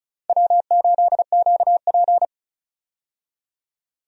Morse code, heard as a keyed tone, about 35 wpm, 700 Hz. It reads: W8QP
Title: USA Call Signs - 4 Characters - 35wpm